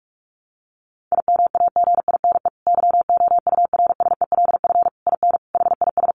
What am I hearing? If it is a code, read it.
Morse code, heard as a keyed tone, about 44 wpm, 700 Hz. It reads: IMAGINE YOURSELF IN HIS
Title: Sentences from Top 1400 Words 45wpm